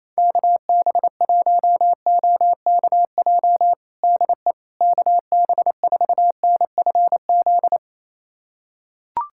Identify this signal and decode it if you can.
Morse code, heard as a keyed tone, about 28 wpm, 700 Hz. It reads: KB1OKJ DE KB4NFZ
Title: USA Callsign DE Callsign 28wpm